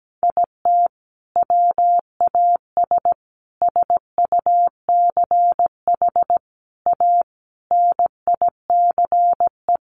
Morse code, heard as a keyed tone, about 17 wpm, 700 Hz. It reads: IT WAS SUCH A NICE
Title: Sentences from Top 1400 Words 17wpm